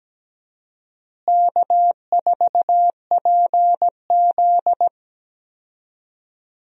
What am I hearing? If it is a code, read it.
Morse code, heard as a keyed tone, about 17 wpm, 700 Hz. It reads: K4PZ
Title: USA Call Signs - 4 Characters - Repeated 3x - 17wpm